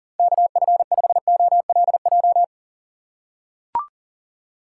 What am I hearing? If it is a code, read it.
Morse code, heard as a keyed tone, about 40 wpm, 700 Hz. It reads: KF5OLJ
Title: USA Call Signs - 6 Characters - 3x Character Spacing - 40wpm